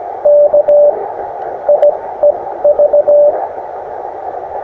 Morse code, heard as a keyed tone, about 17 wpm, spaced as for 12 wpm, 590 Hz, with radio noise. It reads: KIEV
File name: attachments / us7ign a 20m